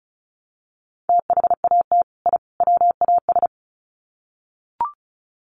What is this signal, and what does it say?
Morse code, heard as a keyed tone, about 34 wpm, 700 Hz. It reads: THAT I WAS
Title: Sets of 3 Words 35wpm